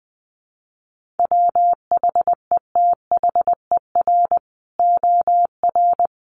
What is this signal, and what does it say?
Morse code, heard as a keyed tone, about 19 wpm, 700 Hz. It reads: WHETHER OR
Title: Sets of 3 Words 20wpm